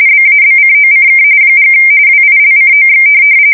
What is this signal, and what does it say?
RTTY, about 45.45 baud. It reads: BABCOCK & WILCOX